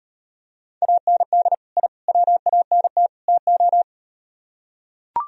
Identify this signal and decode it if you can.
Morse code, heard as a keyed tone, about 38 wpm, 700 Hz. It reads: AND I WANT TO